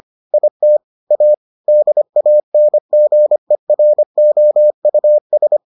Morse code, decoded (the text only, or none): IT A DANGEROUS